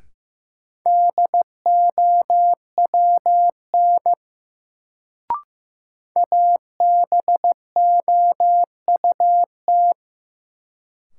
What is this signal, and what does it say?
Morse code, heard as a keyed tone, about 15 wpm, 700 Hz. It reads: DOWN ABOUT
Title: Sets of 2 Words - New Word Drill - 15wpm